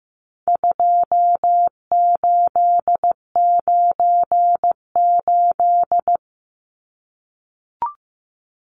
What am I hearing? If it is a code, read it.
Morse code, heard as a keyed tone, about 15 wpm, 700 Hz. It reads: 2898
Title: Numbers - 4-Digit 15wpm